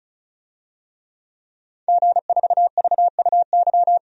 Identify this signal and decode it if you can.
Morse code, heard as a keyed tone, about 34 wpm, 700 Hz. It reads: G4VUY
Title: DXCC-100 Call Signs - 5 Characters - 35wpm